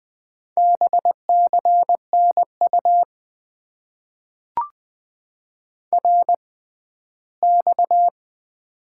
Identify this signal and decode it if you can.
Morse code, heard as a keyed tone, about 20 wpm, spaced as for 14 wpm, 700 Hz. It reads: BCNU RX
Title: Top 90 QSO Elements - 6x Character Spacing - 20wpm